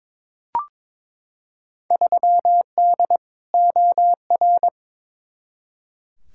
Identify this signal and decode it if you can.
Morse code, heard as a keyed tone, about 22 wpm, 700 Hz. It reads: 3D OR
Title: Field Day Exchanges - 22wpm